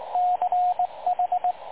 Morse code, heard as a keyed tone, about 20 wpm, 720 Hz, with radio noise. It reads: CH
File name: dlazaawansowanych / nagraniazpasma / websdr recording 3565.0kHz